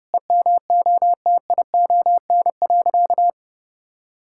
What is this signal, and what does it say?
Morse code, heard as a keyed tone, about 30 wpm, 700 Hz. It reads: EMOTION.
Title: Sentences from Top 1600 Words 30wpm